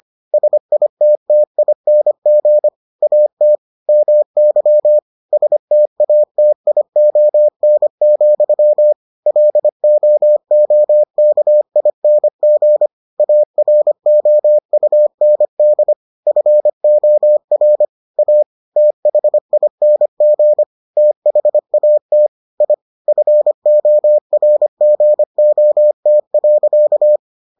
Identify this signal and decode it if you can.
Morse code, heard as a keyed tone, about 25 wpm, 600 Hz. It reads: SITTING AT MY STATION, LOOKING AROUND FOR A THING THAT I FORGOT.